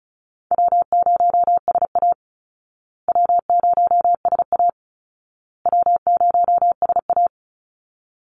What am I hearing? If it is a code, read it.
Morse code, heard as a keyed tone, about 34 wpm, 700 Hz. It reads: W0SA W0SA W0SA
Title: Call Signs Repeated 2x - 35wpm